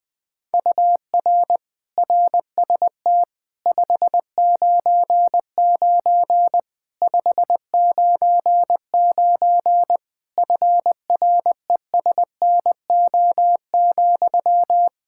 Morse code, heard as a keyed tone, about 21 wpm, 700 Hz. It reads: UR RST 599 599 FRESNO,
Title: Practice QSO - Exchange 1 - Beginning - 20wpm